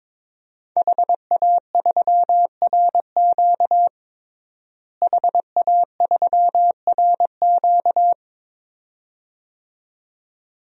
Morse code, heard as a keyed tone, about 22 wpm, 700 Hz. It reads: HA3RQ HA3RQ
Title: DXCC-100 Call Signs - 5 Characters - Repeated 2x - 22wpm